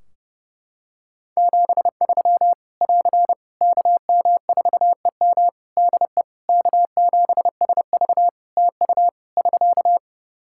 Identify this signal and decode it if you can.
Morse code, heard as a keyed tone, about 30 wpm, 700 Hz. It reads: 73 <AR> KM4EM DE K7SV TU <SK>
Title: Practice QSO - Exchange 4 - Ending - 30wpm